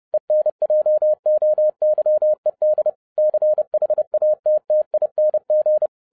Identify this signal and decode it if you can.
Morse code, heard as a keyed tone, about 30 wpm, 610 Hz. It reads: ENJOYED CHATTING